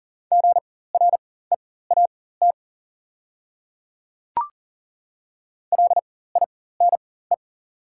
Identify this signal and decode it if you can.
Morse code, heard as a keyed tone, about 39 wpm, spaced as for 19 wpm, 700 Hz. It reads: GREAT LINE
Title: 4x Character Spacing - Top 200 Words - 40wpm